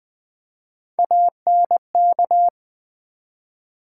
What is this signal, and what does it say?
Morse code, heard as a keyed tone, about 21 wpm, 690 Hz. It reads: ANK
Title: Build A Word - 1 Letter At A Time - 5-Letter Words - 20wpm